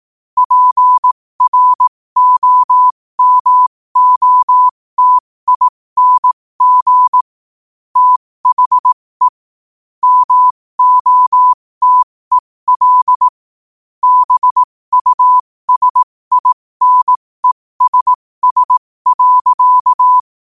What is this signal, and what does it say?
Morse code, heard as a keyed tone, about 18 wpm, spaced as for 15 wpm, 1000 Hz. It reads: PROMOTING THE MOTEL BUSINESS.
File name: qotdfiles / 15wpm